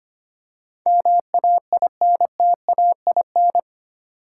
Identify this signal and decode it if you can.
Morse code, heard as a keyed tone, about 25 wpm, 700 Hz. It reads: MAINTAIN
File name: Eight Letter Words - 25wpm